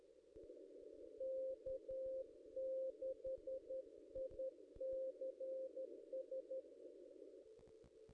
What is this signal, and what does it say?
Morse code, heard as a keyed tone, about 11 wpm, 530 Hz. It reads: K6ICS
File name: contests / ss-cw-k5tr-op-n5rz -1600